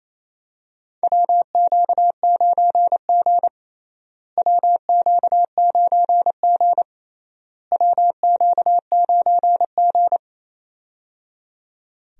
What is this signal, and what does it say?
Morse code, heard as a keyed tone, about 28 wpm, 700 Hz. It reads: WQ9G WQ9G WQ9G
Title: USA Call Signs - 4 Characters - Repeated 3x - 28wpm